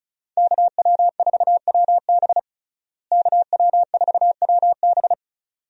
Morse code, heard as a keyed tone, about 34 wpm, 700 Hz. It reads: KW4WB KW4WB
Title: USA Call Signs - 5 Characters - Repeated 2x - 35wpm